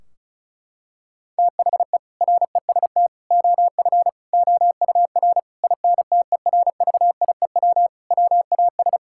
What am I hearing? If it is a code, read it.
Morse code, heard as a keyed tone, about 36 wpm, 700 Hz. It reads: THE REST OF OUR INTERVIEW WAS
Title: Sentences from Top 1600 Words 35wpm